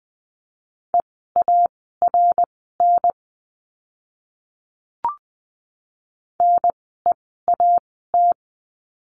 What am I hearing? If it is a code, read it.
Morse code, heard as a keyed tone, about 20 wpm, spaced as for 14 wpm, 700 Hz. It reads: EARN NEAT
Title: 054 Lesson 10f - Letter r - Words 20wpm